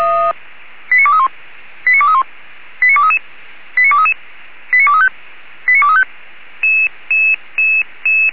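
Five-tone selective call (ZVEI), heard as quick runs of five tones, tones of about 80 ms. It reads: 89231 89231 89233 89233 89235 89235
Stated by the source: ZVEI five-tone selective call 89241 received on an SDR.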